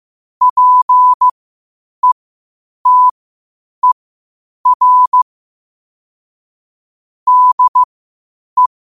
Morse code, heard as a keyed tone, about 15 wpm, spaced as for 8 wpm, 1000 Hz. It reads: PETER DE
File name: qotdfiles / 8wpm